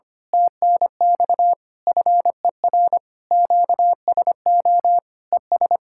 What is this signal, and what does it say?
Morse code, heard as a keyed tone, about 25 wpm, 700 Hz. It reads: TNX FER QSO ES
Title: Speed-Racing Common QSO Phrases - 25wpm